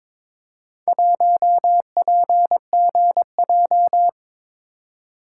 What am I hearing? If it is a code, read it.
Morse code, heard as a keyed tone, about 22 wpm, 700 Hz. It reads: 1PGJ
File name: practice / 22wpm / Four Letter-Number Combinations - 22wpm